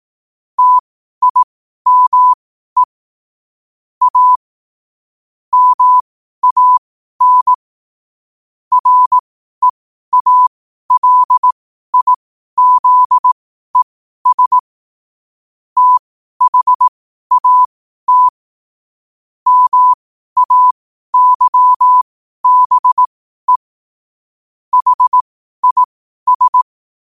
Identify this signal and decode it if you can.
Morse code, heard as a keyed tone, about 18 wpm, spaced as for 12 wpm, 1000 Hz. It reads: TIME A MAN REALIZES THAT MAYBE HIS
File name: qotdfiles / 12wpm